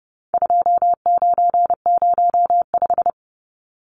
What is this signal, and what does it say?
Morse code, heard as a keyed tone, about 30 wpm, 700 Hz. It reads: 2905
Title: Numbers - 4-Digit 30wpm